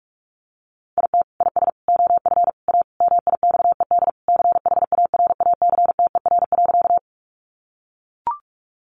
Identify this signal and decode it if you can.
Morse code, heard as a keyed tone, about 44 wpm, 700 Hz. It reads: IT IS OF A MIXED CHARACTER.
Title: Sentences from Top 1500 Words 45wpm